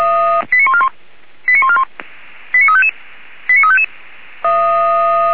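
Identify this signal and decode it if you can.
Five-tone selective call (ZVEI), heard as quick runs of five tones, tones of about 70 ms. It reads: 89141 89141 89355 89355
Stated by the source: Five-tone ZVEI selective call received on SDR, from the feuerwehr-orlamuende.org fire brigade audio server (f85360).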